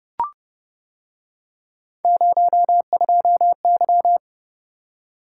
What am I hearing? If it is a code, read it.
Morse code, heard as a keyed tone, about 29 wpm, 700 Hz. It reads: 02Y